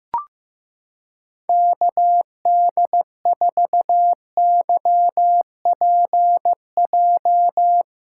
Morse code, heard as a keyed tone, about 15 wpm, 700 Hz. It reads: KD4YPJ
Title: USA Callsign DE Callsign - 6 Characters - 15wpm